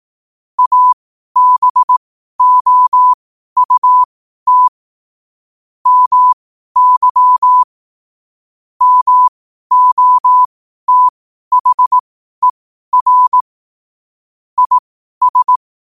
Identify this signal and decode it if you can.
Morse code, heard as a keyed tone, about 18 wpm, spaced as for 12 wpm, 1000 Hz. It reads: ABOUT MY MOTHER IS
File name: qotdfiles / 12wpm